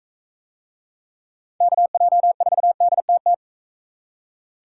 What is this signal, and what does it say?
Morse code, heard as a keyed tone, about 38 wpm, 700 Hz. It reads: KJ4DTT